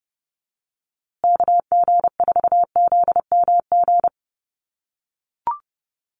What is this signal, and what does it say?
Morse code, heard as a keyed tone, about 30 wpm, 700 Hz. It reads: KG4ZMG
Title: USA Call Signs Repeated 2x 30wpm